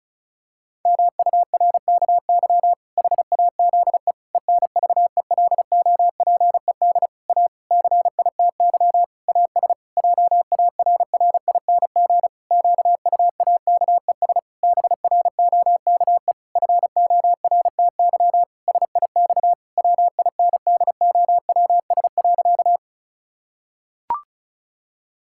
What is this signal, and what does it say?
Morse code, heard as a keyed tone, about 34 wpm, 700 Hz. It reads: MURKY HAZE ENVELOPED A CITY AS JARRING QUAKES BROKE FORTY SIX WINDOWS.
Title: Pangram Challenge v2 - 35wpm